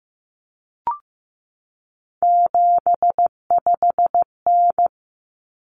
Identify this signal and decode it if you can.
Morse code, heard as a keyed tone, about 15 wpm, 700 Hz. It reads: 75N
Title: Common Cut Numbers - 3-Digit 15wpm